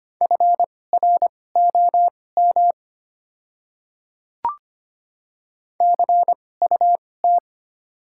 Morse code, heard as a keyed tone, about 25 wpm, spaced as for 18 wpm, 700 Hz. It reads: FROM CUT
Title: 2x Character Spacing - Top 300 Words - 25wpm